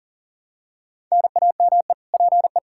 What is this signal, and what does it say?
Morse code, heard as a keyed tone, about 40 wpm, 700 Hz. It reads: NAME PE
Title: Practice QSO - Exchange 1 - Beginning - 40wpm